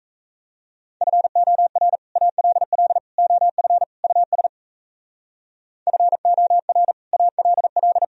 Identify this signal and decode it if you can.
Morse code, heard as a keyed tone, about 36 wpm, 700 Hz. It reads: FOR ALL OF US FOR ALL